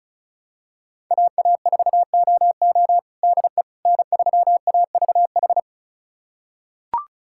Morse code, heard as a keyed tone, about 34 wpm, 700 Hz. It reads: AA4OO DE N3AVH
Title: USA Callsign DE Callsign - 5 Characters - 35wpm